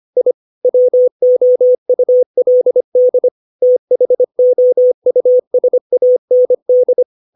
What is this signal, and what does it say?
Morse code, heard as a keyed tone, about 25 wpm, 500 Hz. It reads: I WOULD THOUSAND
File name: 632 words / 632 Words 25WPM